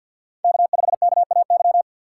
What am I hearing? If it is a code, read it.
Morse code, heard as a keyed tone, about 51 wpm, 700 Hz. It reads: K5KAY
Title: USA Call Signs - 5 Characters - 4x Character Spacing - 50wpm